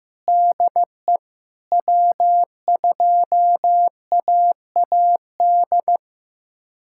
Morse code, heard as a keyed tone, about 15 wpm, 700 Hz. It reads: DE W2AAD
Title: USA Callsign DE Callsign - 5 Characters - 15wpm